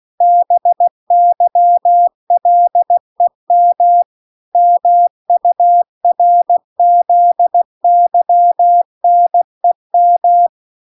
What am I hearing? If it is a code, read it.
Morse code, heard as a keyed tone, about 16 wpm, 700 Hz. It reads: BYLEM MURZYNEM